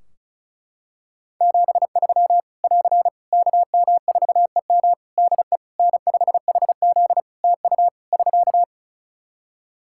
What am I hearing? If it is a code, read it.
Morse code, heard as a keyed tone, about 34 wpm, 700 Hz. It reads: 73 <AR> KM4EM DE N5HZ TU <SK>
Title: Practice QSO - Exchange 4 - Ending - 35wpm